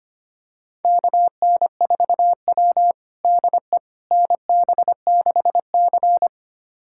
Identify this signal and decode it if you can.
Morse code, heard as a keyed tone, about 25 wpm, 700 Hz. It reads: KN4W DE NB6C
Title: USA Callsign DE Callsign - 4 Characters - 25wpm